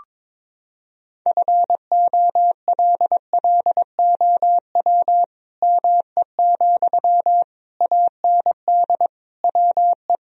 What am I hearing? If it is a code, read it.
Morse code, heard as a keyed tone, about 22 wpm, 700 Hz. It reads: FOLLOW ME, AND WE
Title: Sentences from Top 1000 Words - Review Part 3 - 22wpm